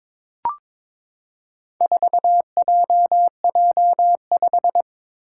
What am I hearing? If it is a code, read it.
Morse code, heard as a keyed tone, about 22 wpm, 700 Hz. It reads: 4JJ5
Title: Four Challenging Letter-Number Combinations 22wpm